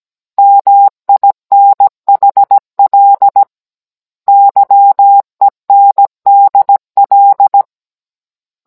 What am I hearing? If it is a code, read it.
Morse code, heard as a keyed tone, about 17 wpm, 800 Hz. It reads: MINHL YENDL
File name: Cours-NR18-16mm